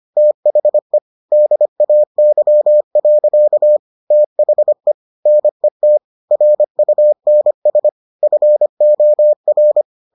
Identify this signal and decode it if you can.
Morse code, heard as a keyed tone, about 25 wpm, 600 Hz. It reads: THE DAY. THE NET RUNS FOR